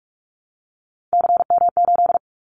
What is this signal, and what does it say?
Morse code, heard as a keyed tone, about 46 wpm, 700 Hz. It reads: CM8
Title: Callsign Prefixes for Top 100 DXCC - 45wpm